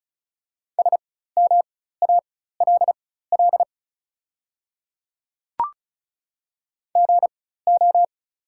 Morse code, heard as a keyed tone, about 36 wpm, spaced as for 16 wpm, 690 Hz. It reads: SMALL GO
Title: 4x Character Spacing - Top 500 Words - 35wpm